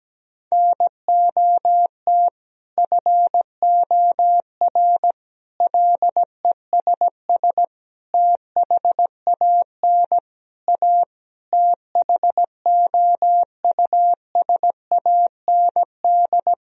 Morse code, heard as a keyed tone, about 17 wpm, 700 Hz. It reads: NOT FOR LESS THAN A THOUSAND
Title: Sentences from Top 1000 Words - Review Part 3 - 17wpm